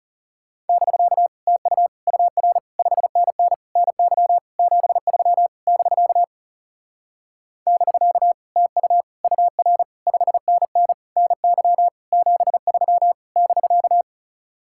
Morse code, heard as a keyed tone, about 40 wpm, 700 Hz. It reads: <BK> TU UR 5NN NY 73 <BK> <BK> TU UR 5NN NY 73 <BK>